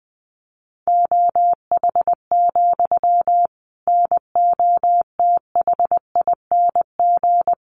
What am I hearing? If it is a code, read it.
Morse code, heard as a keyed tone, about 20 wpm, 700 Hz. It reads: OH, NOTHING
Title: Sentences from Top 700 Words 20wpm